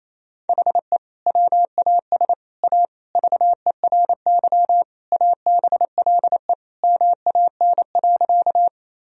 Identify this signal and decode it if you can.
Morse code, heard as a keyed tone, about 28 wpm, 700 Hz. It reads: HE WAS A VERY ABLE MAN.
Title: Sentences from Top 1000 Words - Review Part 2 - 28wpm